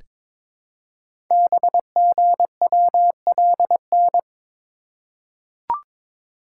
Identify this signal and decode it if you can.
Morse code, heard as a keyed tone, about 22 wpm, 700 Hz. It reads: BGWLN
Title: Code Groups - 5 Letters - 22wpm